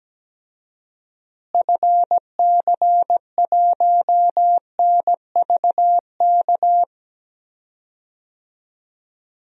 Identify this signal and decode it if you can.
Morse code, heard as a keyed tone, about 17 wpm, 700 Hz. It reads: FC1NVK
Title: DXCC-100 Call Signs - 6 Characters - Repeated 3x - 17wpm